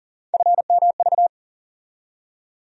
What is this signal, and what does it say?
Morse code, heard as a keyed tone, about 40 wpm, 700 Hz. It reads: FMV
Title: USA Call Sign Postfixes - 3 Letters 40wpm